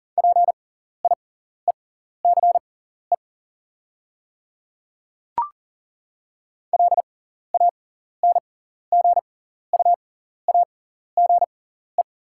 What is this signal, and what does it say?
Morse code, heard as a keyed tone, about 40 wpm, spaced as for 14 wpm, 690 Hz. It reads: PIECE LANGUAGE
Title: 6x Character Spacing - Top 500 Words - 40wpm